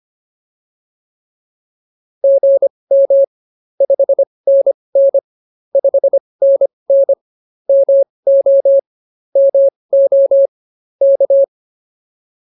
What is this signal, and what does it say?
Morse code, heard as a keyed tone, about 24 wpm, spaced as for 20 wpm, 550 Hz. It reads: GM 5NN 5NN MO MO K